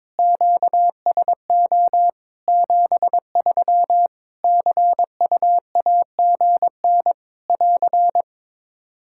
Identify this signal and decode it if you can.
Morse code, heard as a keyed tone, about 22 wpm, 700 Hz. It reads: QSO 73 CUAGN <AR>
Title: Practice QSO - Exchange 4 - Ending - 22wpm